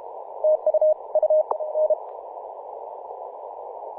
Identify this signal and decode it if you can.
Morse code, heard as a keyed tone, about 33 wpm, 620 Hz. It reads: TU U N
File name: vk9-g7vjr-24-903mhz-cw